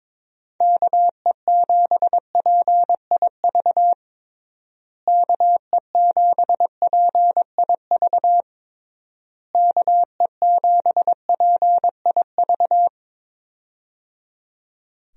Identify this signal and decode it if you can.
Morse code, heard as a keyed tone, about 23 wpm, 700 Hz. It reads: KE7PIV KE7PIV KE7PIV
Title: USA Call Signs - 6 Characters - Repeated 3x - 22wpm